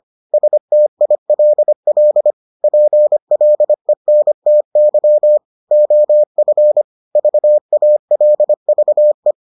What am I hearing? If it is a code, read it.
Morse code, heard as a keyed tone, about 25 wpm, 600 Hz. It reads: STILL PLENTY OF VALVE